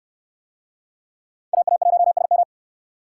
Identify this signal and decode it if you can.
Morse code, heard as a keyed tone, about 69 wpm, 700 Hz. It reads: AA1SM